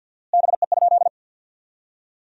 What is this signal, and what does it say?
Morse code, heard as a keyed tone, about 52 wpm, 700 Hz. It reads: BE?
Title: 4x Word-Spacing - Sentences from Top 100 Words - 50wpm